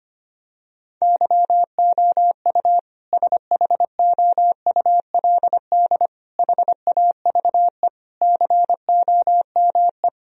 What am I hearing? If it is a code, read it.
Morse code, heard as a keyed tone, about 25 wpm, 700 Hz. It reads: YOU SHOULD HAVE COME
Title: Sentences from Top 1000 Words - Review Part 4 - 25wpm